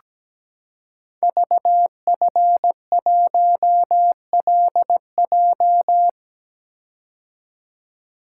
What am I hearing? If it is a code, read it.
Morse code, heard as a keyed tone, about 17 wpm, 690 Hz. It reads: VF1LJ
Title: Code Groups - 5 Letter:Numbers - 17wpm